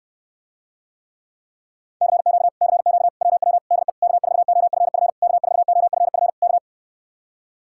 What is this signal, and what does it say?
Morse code, heard as a keyed tone, about 69 wpm, 700 Hz. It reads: CQ CQ WW DE K4OFF K4OFF K